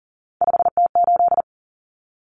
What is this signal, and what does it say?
Morse code, heard as a keyed tone, about 40 wpm, 700 Hz. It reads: 5T8
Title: Common Cut Numbers - 3-Digit 40wpm